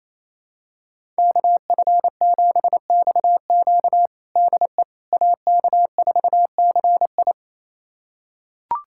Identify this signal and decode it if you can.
Morse code, heard as a keyed tone, about 28 wpm, 700 Hz. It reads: KF7XQ DE AK4CI
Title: USA Callsign DE Callsign - 5 Characters - 28wpm